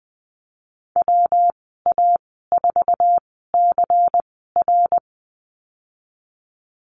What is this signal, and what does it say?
Morse code, heard as a keyed tone, about 19 wpm, spaced as for 14 wpm, 700 Hz. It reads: WA4CR
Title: 140 Lesson 23g - Letter f - Call Signs 20wpm